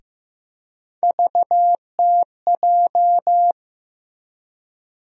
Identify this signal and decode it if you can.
Morse code, heard as a keyed tone, about 15 wpm, 700 Hz. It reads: VTJ
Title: USA Call Sign Postfixes - 3 Letters 15wpm